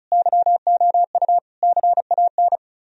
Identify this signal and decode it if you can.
Morse code, heard as a keyed tone, about 43 wpm, 700 Hz. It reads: YOU CAN
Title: Speed-Racing - Sets of 5 Words 35wpm